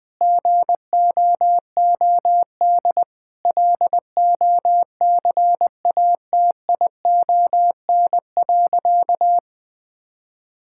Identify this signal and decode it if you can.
Morse code, heard as a keyed tone, about 20 wpm, 700 Hz. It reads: GOOD LOCATION.
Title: Sentences from Top 1000 Words - Review Part 6 - 20wpm